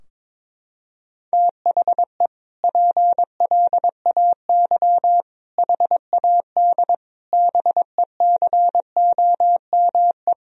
Morse code, heard as a keyed tone, about 22 wpm, 700 Hz. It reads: THE PLAY HAD BECOME
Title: Sentences from Top 1000 Words - Review Part 4 - 22wpm